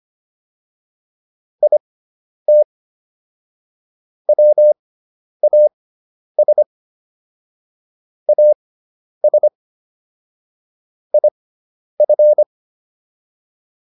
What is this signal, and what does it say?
Morse code, heard as a keyed tone, about 25 wpm, spaced as for 10 wpm, 600 Hz. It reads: IT WAS AS IF